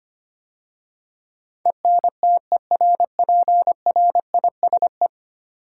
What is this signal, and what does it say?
Morse code, heard as a keyed tone, about 25 wpm, 700 Hz. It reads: ENTERPRISE
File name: 10 Letter Words - First 3 Letters Given - 25wpm